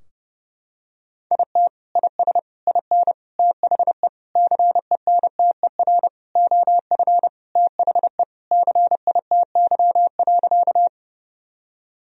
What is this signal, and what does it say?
Morse code, heard as a keyed tone, about 30 wpm, 700 Hz. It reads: IT IS IN THE CENTER OF THE CITY.
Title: Sentences from Top 1000 Words - Review Part 2 - 30wpm